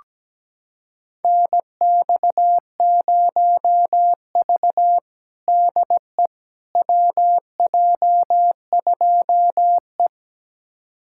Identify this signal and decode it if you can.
Morse code, heard as a keyed tone, about 17 wpm, 700 Hz. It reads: NX0V DE WJ2E
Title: USA Callsign DE Callsign - 4 Characters - 17wpm